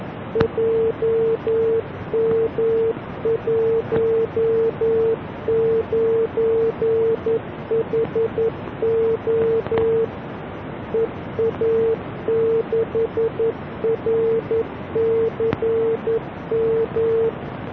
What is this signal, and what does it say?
Morse code, heard as a keyed tone, about 11 wpm, 450 Hz. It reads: JM19HO EA6RCM